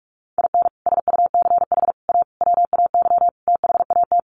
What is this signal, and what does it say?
Morse code, heard as a keyed tone, about 44 wpm, 700 Hz. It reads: IN SUCH A WAY THAT
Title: Sets of 5 Words 45wpm